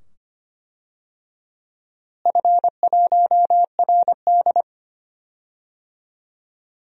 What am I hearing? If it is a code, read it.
Morse code, heard as a keyed tone, about 25 wpm, 700 Hz. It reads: F1RD
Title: DXCC-100 Call Signs - 4 Characters - Repeated 2x - 25wpm